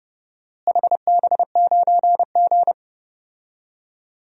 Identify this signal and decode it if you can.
Morse code, heard as a keyed tone, about 30 wpm, 700 Hz. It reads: HB9G
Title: DXCC-100 Call Signs - 4 Characters - Repeated 3x - 30wpm